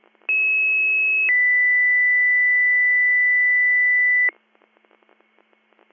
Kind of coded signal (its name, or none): two-tone page